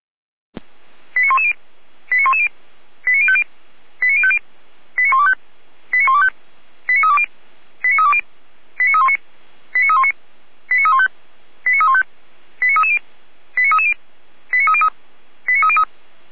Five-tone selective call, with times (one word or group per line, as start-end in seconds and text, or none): 1.2-1.5 89110
2.1-2.5 89110
3.1-3.4 89050
4.0-4.4 89050
5.0-5.3 89125
5.9-6.3 89125
6.9-7.3 89320
7.8-8.2 89320
8.8-9.2 89319
9.7-10.1 89319
10.7-11.1 89315
11.7-12.0 89315
12.6-13.0 89330
13.6-13.9 89330
14.5-14.9 89303
15.5-15.8 89303